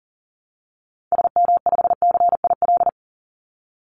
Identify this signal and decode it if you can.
Morse code, heard as a keyed tone, about 40 wpm, 700 Hz. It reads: SM5CIL
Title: Call Signs - 40wpm